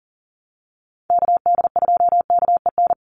Morse code, heard as a keyed tone, about 40 wpm, 700 Hz. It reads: KD2KEN
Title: USA Call Signs 40wpm